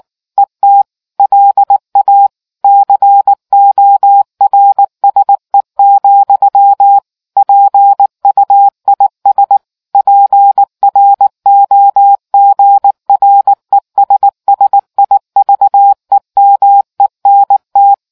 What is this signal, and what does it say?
Morse code, heard as a keyed tone, about 19 wpm, 800 Hz. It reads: ET LA CORSE, PUIS PROGRESSIVEMENT